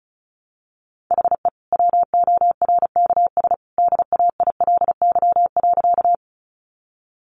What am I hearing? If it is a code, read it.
Morse code, heard as a keyed tone, about 34 wpm, 700 Hz. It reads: HE WORKS DAILY.
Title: Sentences from Top 1100 Words 35wpm